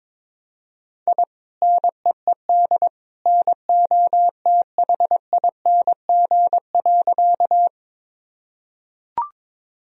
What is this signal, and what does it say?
Morse code, heard as a keyed tone, about 22 wpm, 700 Hz. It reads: I NEED NOTHING.
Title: Sentences from Top 1000 Words - Review Part 1 - 22wpm